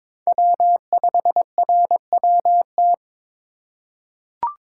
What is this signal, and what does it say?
Morse code, heard as a keyed tone, about 22 wpm, 700 Hz. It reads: W5RWT
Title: USA Call Signs - 5 Characters - 4x Character Spacing - 22wpm